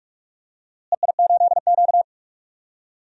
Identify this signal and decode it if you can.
Morse code, heard as a keyed tone, about 42 wpm, 700 Hz. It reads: EI8Q
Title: DXCC-100 Call Signs - 4 Characters - Repeated 2x - 45wpm